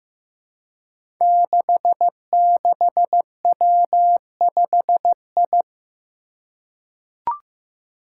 Morse code, heard as a keyed tone, about 15 wpm, 700 Hz. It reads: 66W5I
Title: Five Challenging Letter-Number Combinations 15wpm